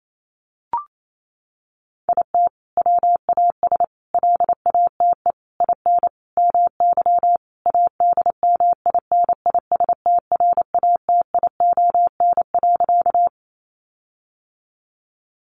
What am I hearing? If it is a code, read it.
Morse code, heard as a keyed tone, about 28 wpm, 700 Hz. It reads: IT WAS LATE IN MY ADMINISTRATION.
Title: Sentences from Top 1300 Words 28wpm